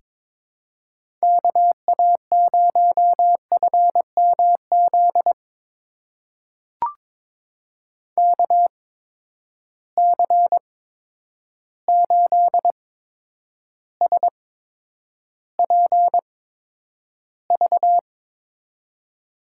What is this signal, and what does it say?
Morse code, heard as a keyed tone, about 22 wpm, spaced as for 9 wpm, 700 Hz. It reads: KA0FMZ KC8SPV
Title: USA Call Signs - 6 Characters - 8x Character Spacing - 22wpm